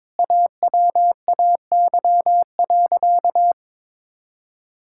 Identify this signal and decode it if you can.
Morse code, heard as a keyed tone, about 22 wpm, 700 Hz. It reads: AWAY.
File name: practice / 22wpm / Sentences with 2-Word Phrases - 22wpm